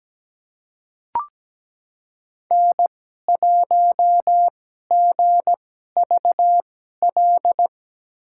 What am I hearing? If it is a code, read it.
Morse code, heard as a keyed tone, about 17 wpm, spaced as for 12 wpm, 700 Hz. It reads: N1GVL
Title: USA Call Signs - 5 Characters - 2x Character Spacing - 17wpm